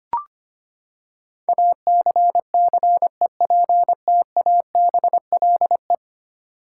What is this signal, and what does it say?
Morse code, heard as a keyed tone, about 25 wpm, 700 Hz. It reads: ACCEPTABLE
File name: Ten Letter Words - 25wpm